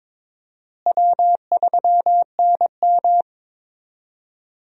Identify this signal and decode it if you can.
Morse code, heard as a keyed tone, about 22 wpm, 700 Hz. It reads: W3NM
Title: USA Call Signs - 4 Characters - Repeated 3x - 22wpm